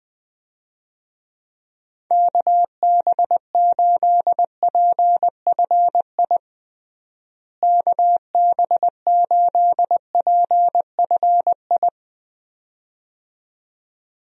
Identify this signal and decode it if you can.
Morse code, heard as a keyed tone, about 20 wpm, 700 Hz. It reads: KB8PFI KB8PFI
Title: USA Call Signs - 6 Characters - Repeated 2x - 20wpm